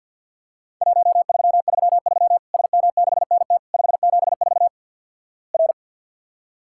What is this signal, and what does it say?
Morse code, heard as a keyed tone, about 50 wpm, 700 Hz. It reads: 1333 SM6NT 574 R
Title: Worked All Europe DX Contest - QTC Exchange with Repeats - 50wpm